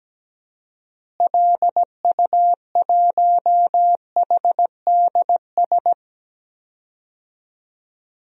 Morse code, heard as a keyed tone, about 17 wpm, 700 Hz. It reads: LU1HDS
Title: DXCC-100 Call Signs - 6 Characters - Repeated 3x - 17wpm